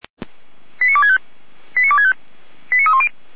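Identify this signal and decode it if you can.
Five-tone selective call (ZVEI), heard as quick runs of five tones, tones of about 70 ms. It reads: 89265 89265 89310
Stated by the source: ZVEI five-tone call received by SDR, from feuerwehr-orlamuende.org (Feuerwehr Orlamünde) archive.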